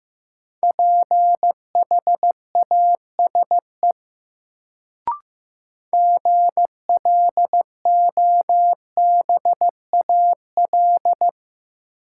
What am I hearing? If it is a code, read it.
Morse code, heard as a keyed tone, about 16 wpm, 700 Hz. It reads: PHASE GLOBAL
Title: Sentences from Top 1200 Words - New Word Drill Alt - 15wpm